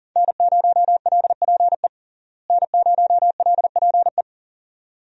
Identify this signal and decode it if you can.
Morse code, heard as a keyed tone, about 40 wpm, 700 Hz. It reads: N0LPE N0LPE
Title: USA Call Signs - 5 Characters - Repeated 2x - 40wpm